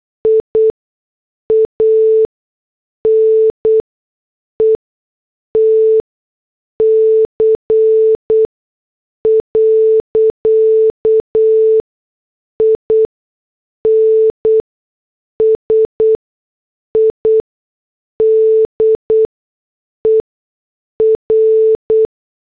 Morse code, heard as a keyed tone, about 8 wpm, spaced as for 6 wpm, 440 Hz. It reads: IANETC. INSIDER